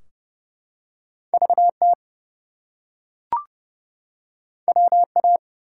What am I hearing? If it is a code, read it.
Morse code, heard as a keyed tone, about 30 wpm, 700 Hz. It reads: VT WA
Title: US States and Canadian Provinces 30wpm